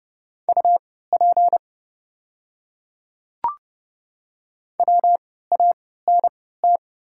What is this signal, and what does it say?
Morse code, heard as a keyed tone, about 30 wpm, spaced as for 17 wpm, 700 Hz. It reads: UP WANT
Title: 3x Character Spacing - Top 200 Words - 30wpm